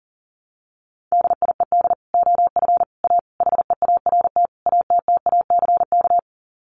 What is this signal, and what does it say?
Morse code, heard as a keyed tone, about 37 wpm, 700 Hz. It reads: DIED OF A HEART ATTACK
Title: Sets of 5 Words Encore2 40wpm